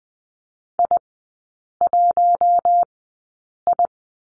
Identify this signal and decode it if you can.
Morse code, heard as a keyed tone, about 20 wpm, spaced as for 8 wpm, 700 Hz. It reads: I1I
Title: 053 Lesson 10e - Letter r - Three Characters 20wpm